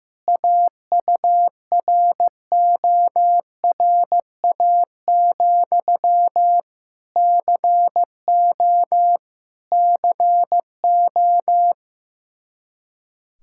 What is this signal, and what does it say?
Morse code, heard as a keyed tone, about 15 wpm, 700 Hz. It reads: AURORA, CO CO
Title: K1USN SST QSO Practice 15wpm